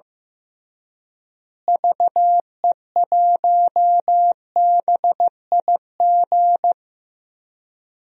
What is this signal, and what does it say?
Morse code, heard as a keyed tone, about 15 wpm, 700 Hz. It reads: VE1BIG
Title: DXCC-100 Call Signs - 6 Characters - Repeated 3x - 15wpm